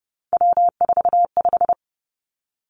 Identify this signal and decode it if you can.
Morse code, heard as a keyed tone, about 30 wpm, 700 Hz. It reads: W45
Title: Three Challenging Letter-Number Combinations 30wpm